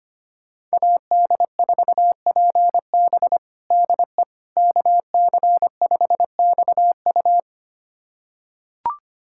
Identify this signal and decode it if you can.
Morse code, heard as a keyed tone, about 25 wpm, 700 Hz. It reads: AD4PB DE KC5XU
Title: USA Callsign DE Callsign - 5 Characters - 25wpm